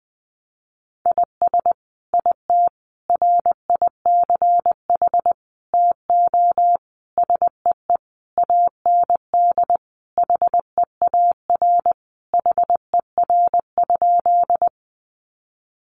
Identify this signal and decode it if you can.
Morse code, heard as a keyed tone, about 20 wpm, 700 Hz. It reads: IS IT RICH TO SEE AND HEAR HER?
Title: Sentences from Top 1100 Words 20wpm